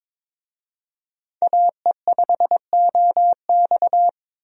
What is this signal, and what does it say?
Morse code, heard as a keyed tone, about 22 wpm, 700 Hz. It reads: AE5OX
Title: USA Call Signs - 5 Characters - 22wpm